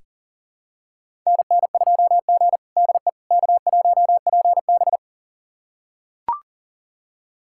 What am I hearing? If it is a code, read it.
Morse code, heard as a keyed tone, about 40 wpm, 700 Hz. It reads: NN2G DE K1PB
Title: USA Callsign DE Callsign - 4 Characters - 40wpm